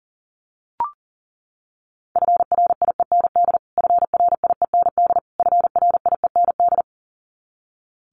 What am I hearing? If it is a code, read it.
Morse code, heard as a keyed tone, about 40 wpm, 700 Hz. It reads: FRIEND FRIEND FRIEND
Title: Top 500 Words Repeated 40wpm